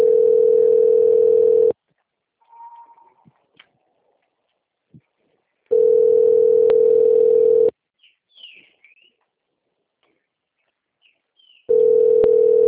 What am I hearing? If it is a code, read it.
Ringback tone (North America), 440 + 480 Hz.
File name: out- - - - - .33077